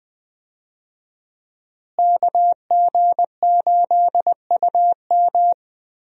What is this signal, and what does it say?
Morse code, heard as a keyed tone, about 20 wpm, 700 Hz. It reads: KG8UM
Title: USA Call Signs - 5 Characters - Repeated 2x - 20wpm